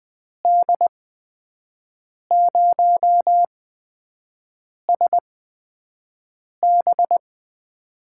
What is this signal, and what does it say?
Morse code, heard as a keyed tone, about 20 wpm, spaced as for 5 wpm, 700 Hz. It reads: D0SB
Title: USA Call Signs - 8x Character Spacing - 20wpm